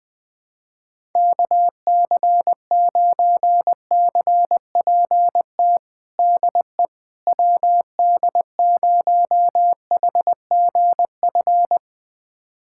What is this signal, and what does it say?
Morse code, heard as a keyed tone, about 21 wpm, 700 Hz. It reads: KC9CPT DE WD0HGF
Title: USA Callsign DE Callsign - 6 Characters - 20wpm